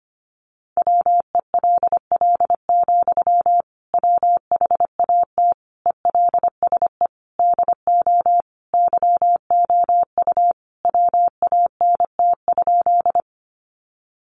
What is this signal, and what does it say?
Morse code, heard as a keyed tone, about 25 wpm, 700 Hz. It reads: WELL, WHAT ELSE DO YOU WANT?
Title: Sentences from Top 800 Words 25wpm